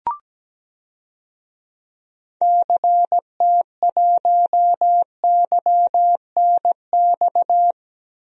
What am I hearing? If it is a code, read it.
Morse code, heard as a keyed tone, about 17 wpm, 700 Hz. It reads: CT1YNX
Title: DXCC-100 Call Signs - 6 Characters - Repeated 3x - 17wpm